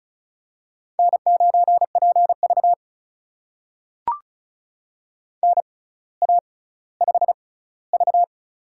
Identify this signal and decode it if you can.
Morse code, heard as a keyed tone, about 36 wpm, spaced as for 18 wpm, 700 Hz. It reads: N9PV NA5V
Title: USA Call Signs - 4 Characters - 6x Character Spacing - 35wpm